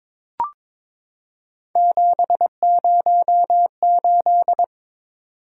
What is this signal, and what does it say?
Morse code, heard as a keyed tone, about 22 wpm, 700 Hz. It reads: 708
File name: practice / 22wpm / Three Letter-Number Combinations - 22wpm